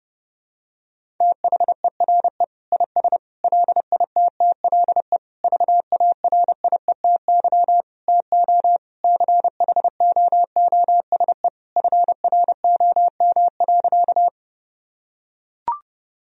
Morse code, heard as a keyed tone, about 32 wpm, 700 Hz. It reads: THERE IS LITTLE VARIETY TO CHOOSE FROM.
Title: Sentences from Top 1000 Words - Review Part 4 - 30wpm